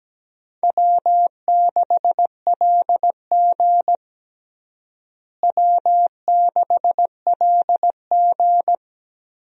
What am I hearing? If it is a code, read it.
Morse code, heard as a keyed tone, about 17 wpm, 700 Hz. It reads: W6LG W6LG
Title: USA Call Signs - 4 Characters - Repeated 3x - 17wpm